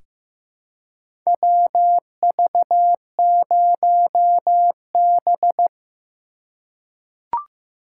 Morse code, heard as a keyed tone, about 15 wpm, 700 Hz. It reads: WV0B
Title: USA Call Signs - 4 Characters - 3x Character Spacing - 15wpm